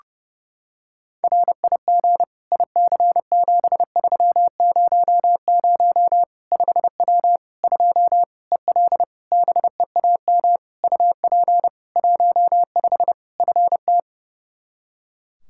Morse code, heard as a keyed tone, about 30 wpm, 700 Hz. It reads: RIG IC7300 5W 2 EL BEAM UP 15 FT
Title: K1USN SST QSO Practice 30wpm